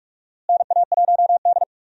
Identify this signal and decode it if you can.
Morse code, heard as a keyed tone, about 41 wpm, 700 Hz. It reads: NA1D
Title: USA Call Signs - 4 Characters - Repeated 3x - 45wpm